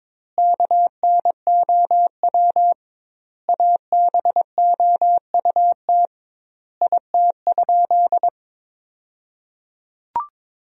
Morse code, heard as a keyed tone, about 22 wpm, 700 Hz. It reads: KNOW ABOUT IT?
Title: 2x Word-Spacing - Sentences from Top 100 Words - 22wpm